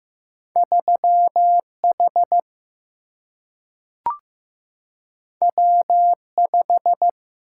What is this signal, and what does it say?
Morse code, heard as a keyed tone, about 15 wpm, 700 Hz. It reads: 3 H W 5
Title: Single-Letter Single-Number 15wpm